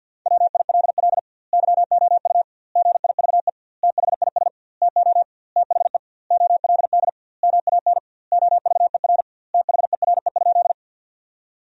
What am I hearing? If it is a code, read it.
Morse code, heard as a keyed tone, about 50 wpm, 700 Hz. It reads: WILL YOU GIVE THIS TO THE OLD MAN OVER THERE?
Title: 2x Word-Spacing - Sentences from Top 200 Words - 50wpm